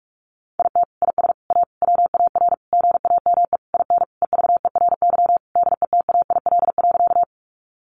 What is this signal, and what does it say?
Morse code, heard as a keyed tone, about 44 wpm, 700 Hz. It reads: IT IS A WAR GAME IN EVERY DETAIL.
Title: Sentences from Top 1500 Words 45wpm